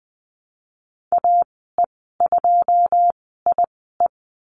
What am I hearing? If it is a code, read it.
Morse code, heard as a keyed tone, about 20 wpm, spaced as for 14 wpm, 700 Hz. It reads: AE2IE
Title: 082 Lesson 14g - Number 2 - Call Signs 20wpm